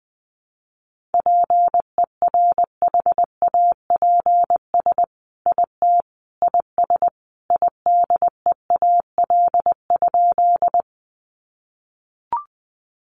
Morse code, heard as a keyed tone, about 20 wpm, 700 Hz. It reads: PERHAPS IT IS IDEAL?
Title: Sentences from Top 1400 Words 20wpm